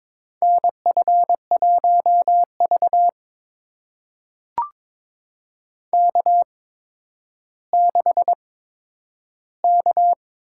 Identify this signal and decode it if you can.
Morse code, heard as a keyed tone, about 22 wpm, spaced as for 11 wpm, 700 Hz. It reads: NF1V K6K
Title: USA Call Signs - 4 Characters - 8x Character Spacing - 22wpm